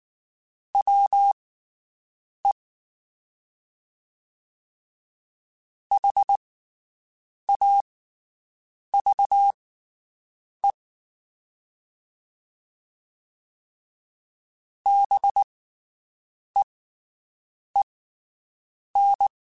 Morse code, heard as a keyed tone, about 19 wpm, spaced as for 7 wpm, 800 Hz. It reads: WE HAVE BEEN
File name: (spaced) / QSOs Spaced-mp3s / WD0BGP 1 @18 sp